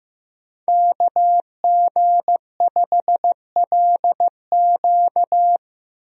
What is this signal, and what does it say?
Morse code, heard as a keyed tone, about 15 wpm, 700 Hz. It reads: KG5LQ
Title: USA Call Signs - 5 Characters - Repeated 3x - 15wpm